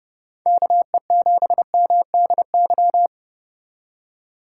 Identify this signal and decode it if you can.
Morse code, heard as a keyed tone, about 30 wpm, 700 Hz. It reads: KE7MDY